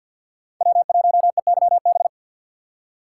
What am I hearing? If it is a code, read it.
Morse code, heard as a keyed tone, about 47 wpm, 700 Hz. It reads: W1EYB
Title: USA Call Signs - 5 Characters - 4x Character Spacing - 50wpm